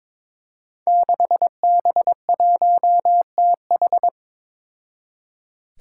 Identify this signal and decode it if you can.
Morse code, heard as a keyed tone, about 22 wpm, 700 Hz. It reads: 6B1TH
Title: Five Challenging Letter-Number Combinations 22wpm